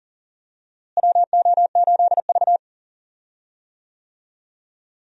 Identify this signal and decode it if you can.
Morse code, heard as a keyed tone, about 39 wpm, 700 Hz. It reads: WO8V
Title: USA Call Signs - 4 Characters - Repeated 2x - 40wpm